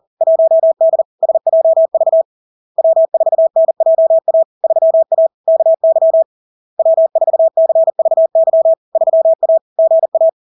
Morse code, heard as a keyed tone, about 40 wpm, 650 Hz. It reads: 1D SJV W4NJA 3A KY W4CVY 3A GA
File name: LARK-FD-2022-Log-in-CW-40wpm-650-Hz